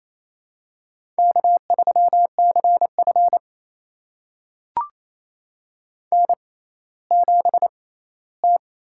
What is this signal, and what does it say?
Morse code, heard as a keyed tone, about 28 wpm, spaced as for 16 wpm, 700 Hz. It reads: K3CF N7T
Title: USA Call Signs - 6x Character Spacing - 28wpm